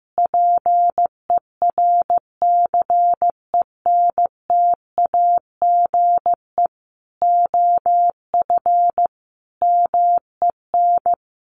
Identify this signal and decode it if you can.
Morse code, heard as a keyed tone, about 15 wpm, 700 Hz. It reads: PERCENTAGE OF MEN
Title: Sentences from Top 1300 Words 15wpm